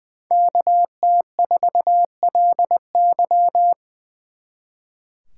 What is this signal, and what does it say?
Morse code, heard as a keyed tone, about 23 wpm, 700 Hz. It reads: KT4LY
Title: Speed Racing - USA 2x2 Call Signs 20wpm